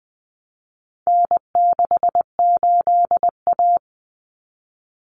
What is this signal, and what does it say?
Morse code, heard as a keyed tone, about 20 wpm, 700 Hz. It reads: N68A
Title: Common Cut Numbers - 4-Digit 20wpm